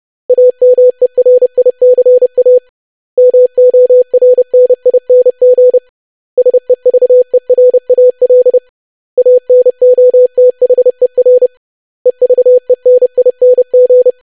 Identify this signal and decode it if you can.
Morse code, heard as a keyed tone, about 30 wpm, 500 Hz. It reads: AMERICA MORNING SEVERAL ANOTHER EVENING
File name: lesson / web / basic500-25wpm